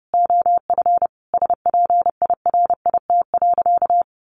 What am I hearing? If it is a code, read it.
Morse code, heard as a keyed tone, about 30 wpm, 700 Hz. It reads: OF SPIRIT.
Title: Sentences from Top 1400 Words 30wpm